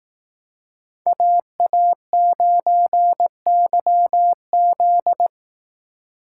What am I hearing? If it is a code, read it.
Morse code, heard as a keyed tone, about 19 wpm, 700 Hz. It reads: AA9YZ
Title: Speed Racing - USA Call Signs 15wpm